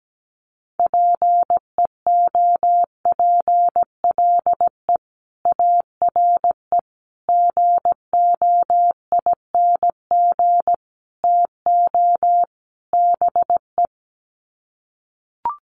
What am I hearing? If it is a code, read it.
Morse code, heard as a keyed tone, about 17 wpm, 700 Hz. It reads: PEOPLE ARE GOING TO BE
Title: Sets of 5 Words Encore 17wpm